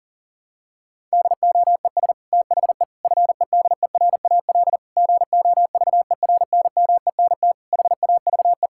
Morse code, heard as a keyed tone, about 40 wpm, 700 Hz. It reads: DOES THE FEDERAL GOVERNMENT HAVE
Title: Sentences from Top 1000 Words - Review Part 5 - 40wpm